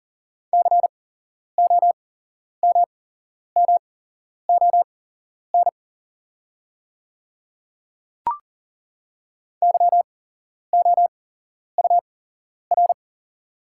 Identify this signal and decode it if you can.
Morse code, heard as a keyed tone, about 40 wpm, spaced as for 11 wpm, 690 Hz. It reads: COMMON YOUR
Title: 8x Character Spacing - Top 500 Words - 40wpm